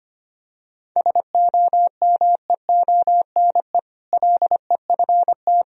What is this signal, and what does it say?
Morse code, heard as a keyed tone, about 25 wpm, 700 Hz. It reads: SOMEONE LEFT
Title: Sentences from Top 1000 Words - Review Part 4 - 25wpm